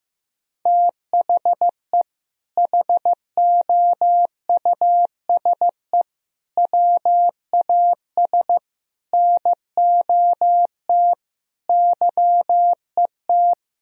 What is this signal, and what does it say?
Morse code, heard as a keyed tone, about 16 wpm, 700 Hz. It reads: THE HOUSE WAS NOT YET
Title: Sentences from Top 1000 Words - Review Part 2 - 15wpm